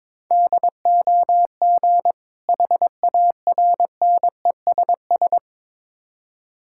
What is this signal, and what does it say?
Morse code, heard as a keyed tone, about 22 wpm, 700 Hz. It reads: DOG HARNESS
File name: Difficult Words with Hints - 22wpm